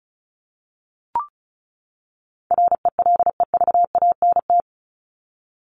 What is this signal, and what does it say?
Morse code, heard as a keyed tone, about 34 wpm, 700 Hz. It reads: RELEVANT
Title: Top 1001-1250 Words 35wpm